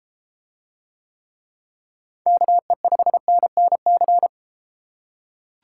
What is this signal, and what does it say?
Morse code, heard as a keyed tone, about 30 wpm, 700 Hz. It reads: KE5NNC